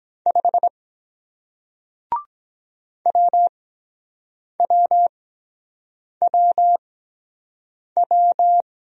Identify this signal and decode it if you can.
Morse code, heard as a keyed tone, about 23 wpm, spaced as for 7 wpm, 700 Hz. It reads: 5 WWWW